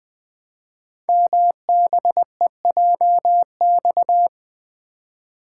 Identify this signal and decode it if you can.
Morse code, heard as a keyed tone, about 21 wpm, 700 Hz. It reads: MBEJX
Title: Code Groups - 5 Letters - 20wpm